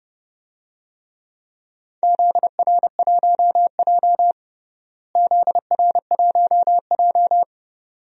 Morse code, heard as a keyed tone, about 32 wpm, 700 Hz. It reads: ZR1J ZR1J
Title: DXCC-100 Call Signs - 4 Characters - Repeated 2x - 30wpm